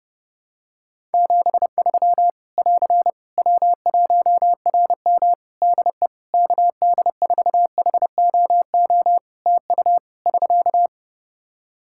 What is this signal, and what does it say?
Morse code, heard as a keyed tone, about 28 wpm, 700 Hz. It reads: 73 <AR> W1RM DE KD4HOO TU <SK>
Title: Practice QSO - Exchange 4 - Ending - 30wpm